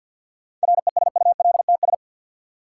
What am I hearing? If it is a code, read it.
Morse code, heard as a keyed tone, about 49 wpm, 700 Hz. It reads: RESULTS
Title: Sets of 4 Words - New Word Drill - 50wpm